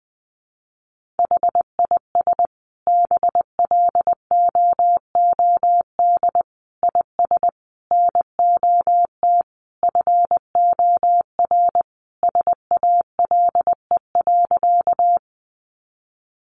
Morse code, heard as a keyed tone, about 20 wpm, 700 Hz. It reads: HIS BLOOD IS NOT FOR SALE.
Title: Sentences from Top 1400 Words 20wpm